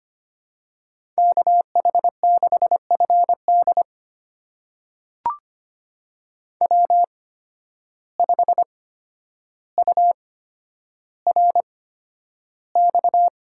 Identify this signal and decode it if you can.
Morse code, heard as a keyed tone, about 25 wpm, spaced as for 11 wpm, 700 Hz. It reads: KH6FD W5URX
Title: USA Call Signs - 5 Characters - 8x Character Spacing - 25wpm